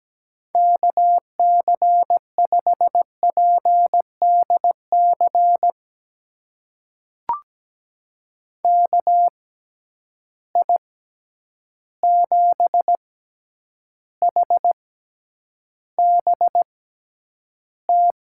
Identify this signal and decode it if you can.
Morse code, heard as a keyed tone, about 17 wpm, spaced as for 9 wpm, 700 Hz. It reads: KC5PDC KI7HBT
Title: USA Call Signs - 6 Characters - 6x Character Spacing - 17wpm